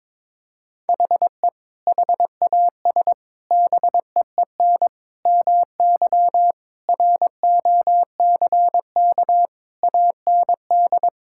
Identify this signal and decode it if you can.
Morse code, heard as a keyed tone, about 22 wpm, 700 Hz. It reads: HE HAS BEEN MY ROCK AND
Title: Sentences from Top 1000 Words - Review Part 5 - 22wpm